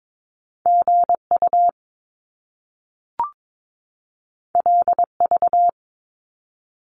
Morse code, heard as a keyed tone, about 22 wpm, 700 Hz. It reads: GU LV
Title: Top Two Letter Combinations in Words 22wpm